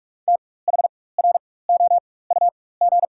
Morse code, heard as a keyed tone, about 44 wpm, spaced as for 21 wpm, 700 Hz. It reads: THROUG
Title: 4x Character Spacing - Top 500 Words - 45wpm